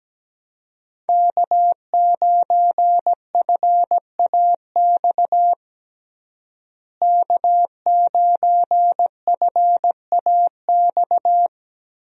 Morse code, heard as a keyed tone, about 17 wpm, 700 Hz. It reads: K9FAX K9FAX
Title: USA Call Signs - 5 Characters - Repeated 3x - 17wpm